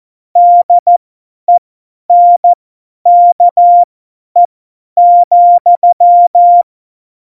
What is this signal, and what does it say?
Morse code, heard as a keyed tone, about 14 wpm, spaced as for 10 wpm, 700 Hz. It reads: DENKE,
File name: Morsekurs / Witze / Witz 22. Januar 2019, 50 F70